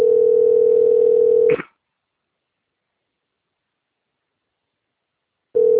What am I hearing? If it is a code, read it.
Ringback tone (North America), 440 + 480 Hz.